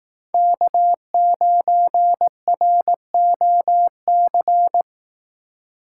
Morse code, heard as a keyed tone, about 18 wpm, 700 Hz. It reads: K9ROC
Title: Speed Racing - USA 1x3 Call Signs 15wpm